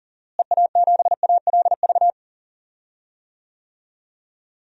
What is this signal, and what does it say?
Morse code, heard as a keyed tone, about 40 wpm, 700 Hz. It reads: EA7ALV
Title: DXCC-100 Call Signs - 6 Characters - Repeated 3x - 40wpm